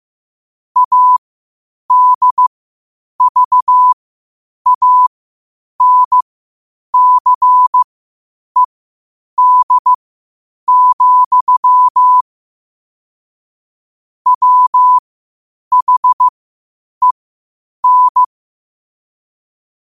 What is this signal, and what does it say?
Morse code, heard as a keyed tone, about 15 wpm, spaced as for 8 wpm, 1000 Hz. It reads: ADVANCED, WHEN
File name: qotdfiles / 8wpm